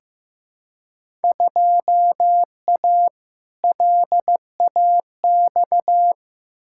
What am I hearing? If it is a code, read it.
Morse code, heard as a keyed tone, about 15 wpm, 700 Hz. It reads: 2A LAX
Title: Field Day Exchanges - 15wpm